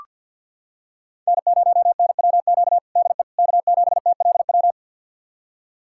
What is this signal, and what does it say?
Morse code, heard as a keyed tone, about 45 wpm, 690 Hz. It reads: N0NWQ DE K7TLW
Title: USA Callsign DE Callsign - 5 Characters - 50wpm